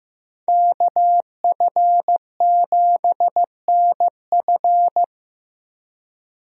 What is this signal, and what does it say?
Morse code, heard as a keyed tone, about 15 wpm, 700 Hz. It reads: KF7NF
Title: USA Call Signs - 5 Characters - Repeated 3x - 15wpm